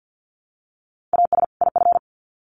Morse code, heard as a keyed tone, about 50 wpm, 700 Hz. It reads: AS IF
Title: Sets of 2 Words 50wpm